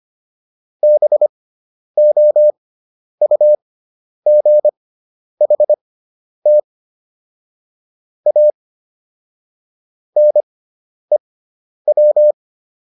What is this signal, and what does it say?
Morse code, heard as a keyed tone, about 25 wpm, spaced as for 10 wpm, 600 Hz. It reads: BOUGHT A NEW